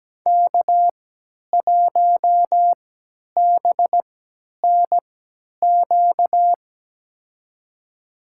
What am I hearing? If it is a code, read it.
Morse code, heard as a keyed tone, about 17 wpm, spaced as for 10 wpm, 700 Hz. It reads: K1BNQ
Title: USA Call Signs - 5 Characters - 3x Character Spacing - 17wpm